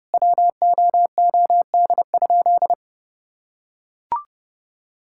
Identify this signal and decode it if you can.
Morse code, heard as a keyed tone, about 32 wpm, 700 Hz. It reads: WOOD?
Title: Sentences from Top 1600 Words 30wpm